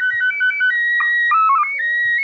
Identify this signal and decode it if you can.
SSTV (slow-scan television), PD 180.
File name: spazio / ISS 145 800MHz SSTV 3